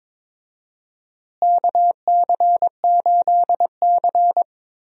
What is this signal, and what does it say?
Morse code, heard as a keyed tone, about 22 wpm, 700 Hz. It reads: KC8C
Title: USA Call Signs - 4 Characters - Repeated 3x - 22wpm